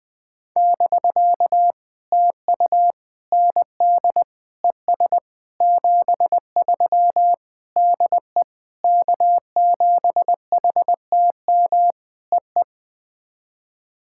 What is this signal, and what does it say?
Morse code, heard as a keyed tone, about 21 wpm, 700 Hz. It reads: <BK> TU ND ES 73 DE K7HTM EE
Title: POTA QSO Practice - 20wpm